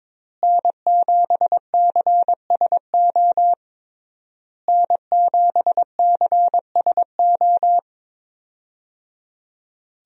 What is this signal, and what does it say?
Morse code, heard as a keyed tone, about 22 wpm, 700 Hz. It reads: N7CSO N7CSO
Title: USA Call Signs - 5 Characters - Repeated 3x - 22wpm